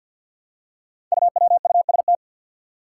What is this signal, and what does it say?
Morse code, heard as a keyed tone, about 49 wpm, 700 Hz. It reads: UWUST
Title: Five Challenging Letter-Number Combinations 50wpm